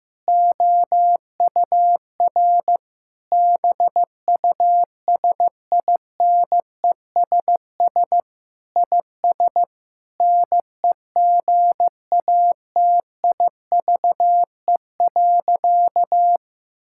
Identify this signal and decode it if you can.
Morse code, heard as a keyed tone, about 15 wpm, 700 Hz. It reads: OUR BUSINESS IS NEGATIVE.
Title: Sentences from Top 1000 Words - Review Part 6 - 15wpm